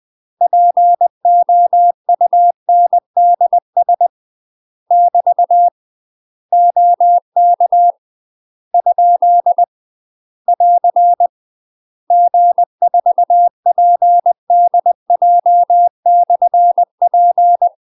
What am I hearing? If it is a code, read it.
Morse code, heard as a keyed tone, about 20 wpm, 700 Hz. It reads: POUNDS <BT> OK ? <AR> G4PDJ/P
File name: UK MASTER QSOs TRAINING / 20WPM / UK MASTER QSOs 0038